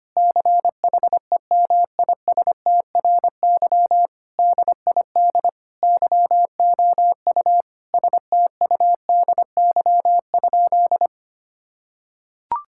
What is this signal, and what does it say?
Morse code, heard as a keyed tone, about 25 wpm, 700 Hz. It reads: CHEMISTRY DID YOU STUDY?
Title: Sentences from Top 1000 Words - Review Part 6 - 25wpm